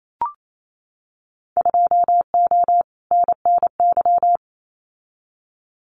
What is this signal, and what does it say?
Morse code, heard as a keyed tone, about 28 wpm, 700 Hz. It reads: 2O NNY
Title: Winter Field Day Exchanges 28wpm